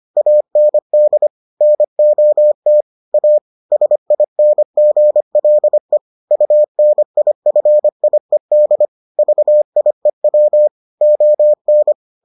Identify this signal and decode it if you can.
Morse code, heard as a keyed tone, about 25 wpm, 600 Hz. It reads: AND NOT A SINGLE UNIFIED VIEW ON